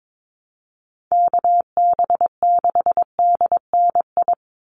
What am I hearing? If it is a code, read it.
Morse code, heard as a keyed tone, about 22 wpm, 700 Hz. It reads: KB6DNI
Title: USA Call Signs 22wpm